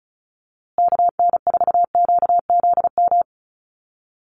Morse code, heard as a keyed tone, about 34 wpm, 700 Hz. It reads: KN4QZM
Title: USA Call Signs Repeated 3x 35wpm